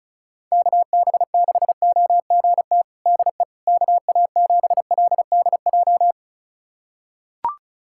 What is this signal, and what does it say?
Morse code, heard as a keyed tone, about 36 wpm, 700 Hz. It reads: KB6OGT DE KA7LDJ
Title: USA Callsign DE Callsign - 6 Characters - 35wpm